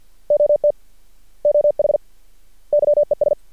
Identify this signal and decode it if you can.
Morse code, heard as a keyed tone, about 48 wpm, 590 Hz. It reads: OT OH YES
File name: archived / CRSnet